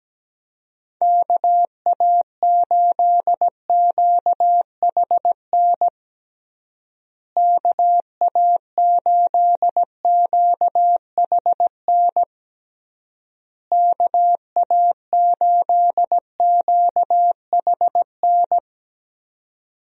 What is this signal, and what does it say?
Morse code, heard as a keyed tone, about 17 wpm, 700 Hz. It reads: KA8QHN KA8QHN KA8QHN
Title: USA Call Signs - 6 Characters - Repeated 3x - 17wpm